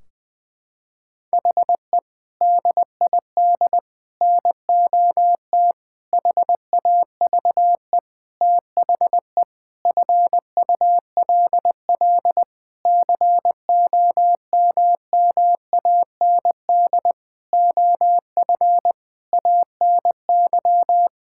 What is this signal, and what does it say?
Morse code, heard as a keyed tone, about 21 wpm, 700 Hz. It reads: HE DID NOT HAVE THE FULL COMMAND OF ANY
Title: Sentences from Top 1600 Words 20wpm